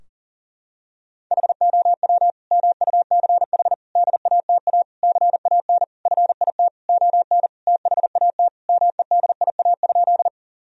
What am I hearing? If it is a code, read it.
Morse code, heard as a keyed tone, about 37 wpm, 700 Hz. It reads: HOW MUCH DATA CAN FIT ON THAT MEDIA?
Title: Sentences from Top 1000 Words - Review Part 4 - 40wpm